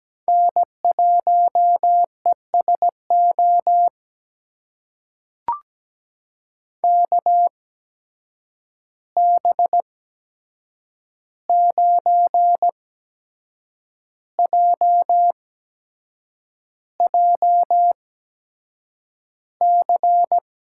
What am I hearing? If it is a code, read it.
Morse code, heard as a keyed tone, about 17 wpm, spaced as for 7 wpm, 700 Hz. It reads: N1ESO KB9JJC
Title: USA Call Signs - 8x Character Spacing - 17wpm